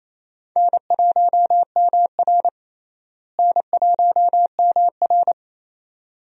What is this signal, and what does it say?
Morse code, heard as a keyed tone, about 28 wpm, 700 Hz. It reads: N1MR N1MR
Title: USA Call Signs - 4 Characters - Repeated 2x - 28wpm